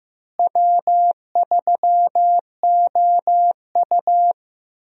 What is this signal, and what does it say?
Morse code, heard as a keyed tone, about 15 wpm, 700 Hz. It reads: W3OU
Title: USA Call Signs - 4 Characters - Repeated 2x - 15wpm